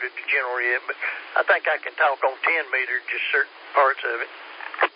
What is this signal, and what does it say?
Emergency Alert System (SAME) tones, heard as harsh buzzing bursts, with speech. It is heard from 0.8 to 4.3 s.